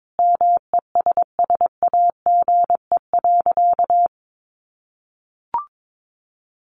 Morse code, heard as a keyed tone, about 22 wpm, 700 Hz. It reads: MESSAGE.
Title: Sentences from Top 1100 Words 22wpm